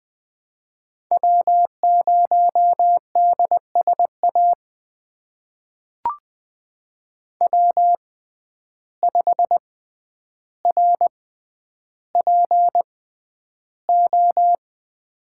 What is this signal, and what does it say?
Morse code, heard as a keyed tone, about 20 wpm, spaced as for 10 wpm, 700 Hz. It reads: W0DSA W5RPO
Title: USA Call Signs - 5 Characters - 6x Character Spacing - 20wpm